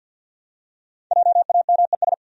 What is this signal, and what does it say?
Morse code, heard as a keyed tone, about 53 wpm, 700 Hz. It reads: JAMES
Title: USA Common Names - 50wpm